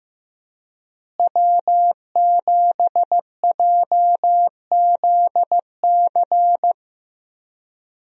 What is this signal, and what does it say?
Morse code, heard as a keyed tone, about 15 wpm, 700 Hz. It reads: W7JZC
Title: USA Call Signs - 5 Characters - Repeated 2x - 15wpm